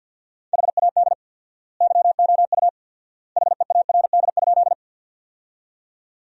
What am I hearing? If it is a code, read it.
Morse code, heard as a keyed tone, about 50 wpm, 700 Hz. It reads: HAD YOU HEARD?
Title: 4x Word-Spacing - Sentences from Top 400 Words - 50wpm